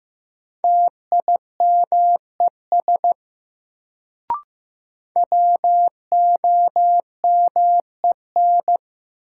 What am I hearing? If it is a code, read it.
Morse code, heard as a keyed tone, about 15 wpm, 700 Hz. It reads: TIMES WOMEN
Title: Sets of 3 Words - New Word Drill - 15wpm